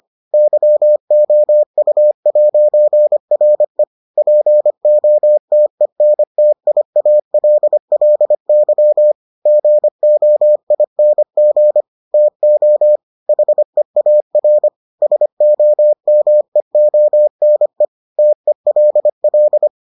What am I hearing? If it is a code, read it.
Morse code, heard as a keyed tone, about 25 wpm, 600 Hz. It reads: YOU'RE POTENTIALLY GOING TO HEAR SOMEONE TELL